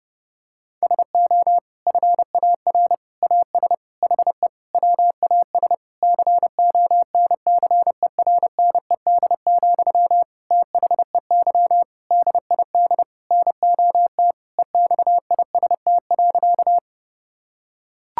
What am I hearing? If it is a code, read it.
Morse code, heard as a keyed tone, about 30 wpm, 700 Hz. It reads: SO FAR AS HE WAS CONCERNED, THEY DID NOT EXIST.
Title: Sentences from Top 1000 Words - Review Part 6 - 30wpm